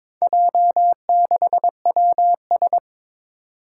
Morse code, heard as a keyed tone, about 22 wpm, 700 Hz. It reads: J6WS
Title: Four Challenging Letter-Number Combinations 22wpm